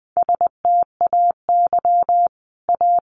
Morse code, heard as a keyed tone, about 20 wpm, 700 Hz. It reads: STAY A
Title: Sentences from Top 900 Words 20wpm